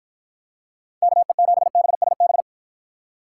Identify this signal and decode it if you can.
Morse code, heard as a keyed tone, about 54 wpm, 700 Hz. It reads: KE7BSB